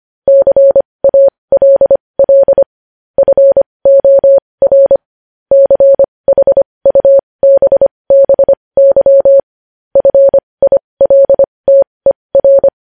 Morse code, heard as a keyed tone, about 25 wpm, spaced as for 20 wpm, 550 Hz. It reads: CALL FOR CHUBBY FILTER
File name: podcast / News headlines-1x-25x20